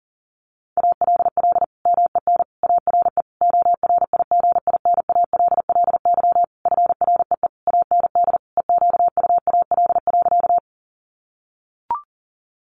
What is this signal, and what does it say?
Morse code, heard as a keyed tone, about 40 wpm, 700 Hz. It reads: ALL MEN ARE ORIGINALLY FREE AND EQUAL.
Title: Sentences from Top 1500 Words 40wpm